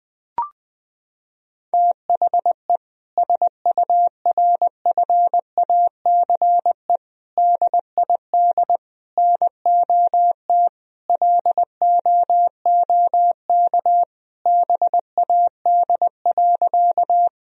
Morse code, heard as a keyed tone, about 20 wpm, 700 Hz. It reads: THE SURFACE DID NOT LOOK BAD.
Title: Sentences from Top 1000 Words - Review Part 3 - 20wpm